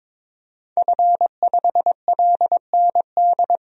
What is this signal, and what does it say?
Morse code, heard as a keyed tone, about 21 wpm, 700 Hz. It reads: F5LND
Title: DXCC-100 Call Signs - 5 Characters - 22wpm